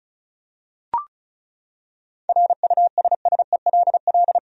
Morse code, heard as a keyed tone, about 34 wpm, 700 Hz. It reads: RUSSELL
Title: USA Common Names - 35wpm